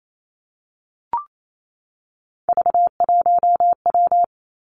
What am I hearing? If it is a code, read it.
Morse code, heard as a keyed tone, about 28 wpm, 690 Hz. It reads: V1W
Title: Three Challenging Letter-Number Combinations 28wpm